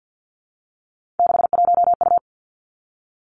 Morse code, heard as a keyed tone, about 50 wpm, 700 Hz. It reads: 61U
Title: Three Challenging Letter-Number Combinations 50wpm